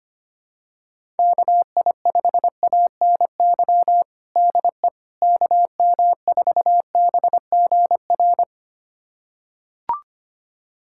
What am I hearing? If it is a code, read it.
Morse code, heard as a keyed tone, about 25 wpm, 700 Hz. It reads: KI5ANY DE KM4BGR
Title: USA Callsign DE Callsign 25wpm